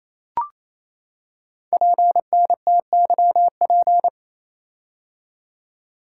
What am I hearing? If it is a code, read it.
Morse code, heard as a keyed tone, about 28 wpm, 700 Hz. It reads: PNTYP
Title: Code Groups - 5 Letter:Numbers - 28wpm